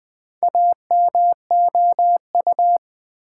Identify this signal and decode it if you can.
Morse code, heard as a keyed tone, about 20 wpm, 700 Hz. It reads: AMOU
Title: Build A Word - 2 Letters At A Time - 6-Letter Words - 20wpm